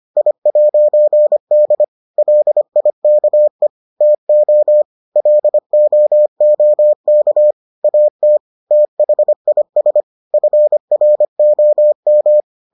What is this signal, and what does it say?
Morse code, heard as a keyed tone, about 25 wpm, 600 Hz. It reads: I'D LIKE TO LOOK AT THIS FROM